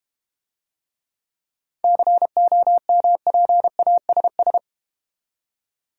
Morse code, heard as a keyed tone, about 32 wpm, 700 Hz. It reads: COMPASS